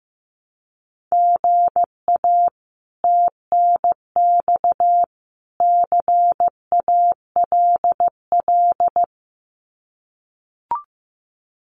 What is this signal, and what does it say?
Morse code, heard as a keyed tone, about 15 wpm, 700 Hz. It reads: GA TNX CALL
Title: Common QSO Phrases 15wpm